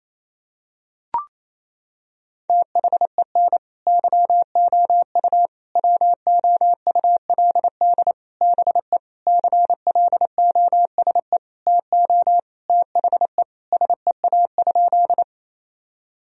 Morse code, heard as a keyed tone, about 28 wpm, 700 Hz. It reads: THEN YOU WOULD BE CLOSE TO THE SEA?
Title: Sentences from Top 1000 Words - Review Part 1 - 28wpm